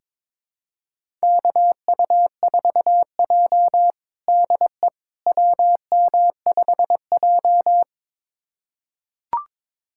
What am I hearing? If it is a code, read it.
Morse code, heard as a keyed tone, about 22 wpm, 700 Hz. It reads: KU4J DE WM5J
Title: USA Callsign DE Callsign - 4 Characters - 22wpm